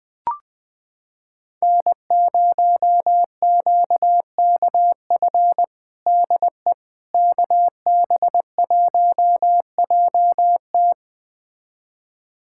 Morse code, heard as a keyed tone, about 21 wpm, 700 Hz. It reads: N0QKF DE KB1JT
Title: USA Callsign DE Callsign - 5 Characters - 20wpm